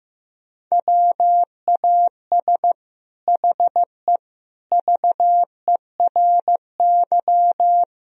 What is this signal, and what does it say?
Morse code, heard as a keyed tone, about 15 wpm, 700 Hz. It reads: WAS HE VERY
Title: Sentences from Top 1000 Words - Review Part 3 - 15wpm